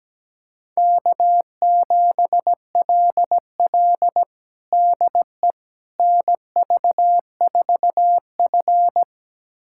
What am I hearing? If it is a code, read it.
Morse code, heard as a keyed tone, about 17 wpm, 700 Hz. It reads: K7LL DE NV4F
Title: USA Callsign DE Callsign - 4 Characters - 17wpm